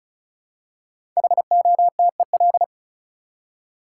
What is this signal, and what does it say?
Morse code, heard as a keyed tone, about 34 wpm, 700 Hz. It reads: HOTEL
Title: Five Letter Words - 35wpm